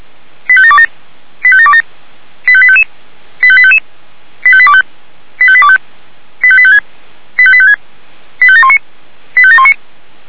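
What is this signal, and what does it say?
Five-tone selective call (ZVEI), heard as quick runs of five tones, tones of about 80 ms. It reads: 85727 85727 85855 85855 85825 85825 85856 85856 85719 85719
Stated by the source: ZVEI five-tone selective call 85854, received by SDR.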